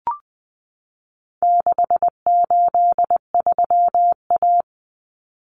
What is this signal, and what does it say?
Morse code, heard as a keyed tone, about 20 wpm, 700 Hz. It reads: 683A
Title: Common Cut Numbers - 4-Digit 20wpm